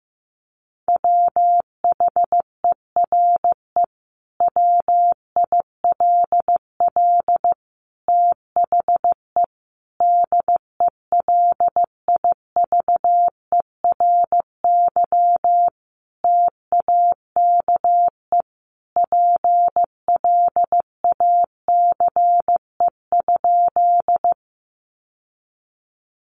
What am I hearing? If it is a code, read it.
Morse code, heard as a keyed tone, about 15 wpm, 700 Hz. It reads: WHERE WILL THE DELIVERY TAKE PLACE?
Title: Sentences from Top 1500 Words 15wpm